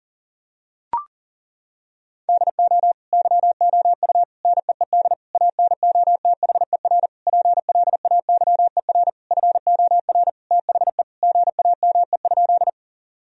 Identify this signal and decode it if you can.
Morse code, heard as a keyed tone, about 40 wpm, 700 Hz. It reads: DO YOU NEED ANOTHER PLAYER FOR THE GAME?
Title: Sentences from Top 1000 Words - Review Part 4 - 40wpm